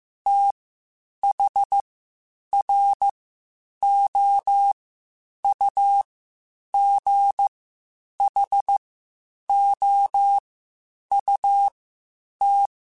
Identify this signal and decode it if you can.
Morse code, heard as a keyed tone, about 15 wpm, spaced as for 8 wpm, 800 Hz. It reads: THROUGHOUT
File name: Spaced Characters / Spaced Characters-mp3s / Weather 5 12 sp @13